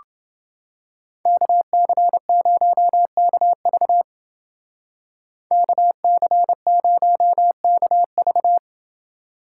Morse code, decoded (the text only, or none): KC0KV KC0KV